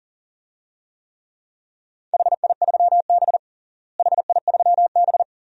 Morse code, heard as a keyed tone, about 40 wpm, 700 Hz. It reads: HI3B HI3B
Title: DXCC-100 Call Signs - 4 Characters - Repeated 2x - 40wpm